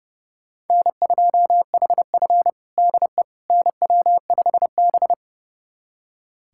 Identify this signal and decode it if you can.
Morse code, heard as a keyed tone, about 30 wpm, 700 Hz. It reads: N2HF DE NW5B
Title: USA Callsign DE Callsign - 4 Characters - 30wpm